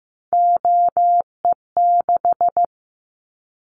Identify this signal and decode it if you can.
Morse code, heard as a keyed tone, about 15 wpm, 700 Hz. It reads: OE6
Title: Callsign Prefixes for Top 50 DXCC - 15wpm